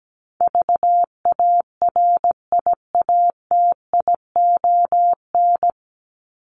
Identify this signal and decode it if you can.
Morse code, heard as a keyed tone, about 17 wpm, 700 Hz. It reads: VARIATION
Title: Top 1501-1750 Words 17wpm